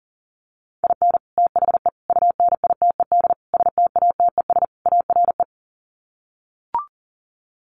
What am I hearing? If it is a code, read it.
Morse code, heard as a keyed tone, about 37 wpm, 700 Hz. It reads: IN THE UNITED STATES ARE
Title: Sets of 5 Words Encore 40wpm